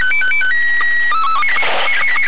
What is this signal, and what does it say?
SSTV (slow-scan television), Robot 36.
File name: Arissat1-05aug02-56msk- sstv